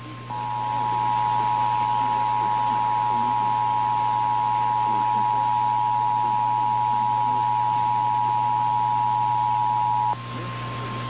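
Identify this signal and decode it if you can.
Emergency Alert System (SAME) tones (the 853 and 960 Hz attention tone), with noise. It is heard from 0.2 to 10.2 s.
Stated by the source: EAS SAME alert tones logged by station KNCYKBIE.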